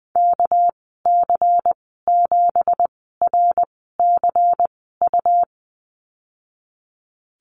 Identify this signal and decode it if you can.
Morse code, heard as a keyed tone, about 20 wpm, spaced as for 15 wpm, 700 Hz. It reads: KC7RCU
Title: 238 Lesson 38g - Letter q - Call Signs 20wpm